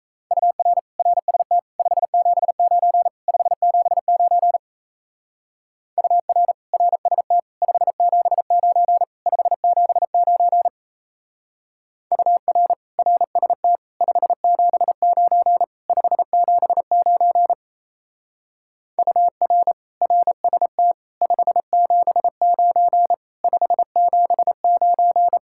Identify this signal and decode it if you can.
Morse code, heard as a keyed tone, about 35 wpm, 700 Hz. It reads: UR RST 579 579 UR RST 579 579 UR RST 579 579 UR RST 579 579
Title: Speed-Racing Common QSO Phrases - 28wpm